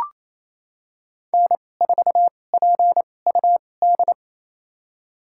Morse code, heard as a keyed tone, about 28 wpm, spaced as for 20 wpm, 700 Hz. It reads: N4PUD
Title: USA Call Signs - 2x Character Spacing - 28wpm